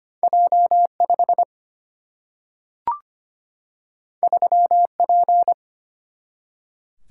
Morse code, heard as a keyed tone, about 25 wpm, 700 Hz. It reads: J5 3P
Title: Single-Letter Single-Number 25wpm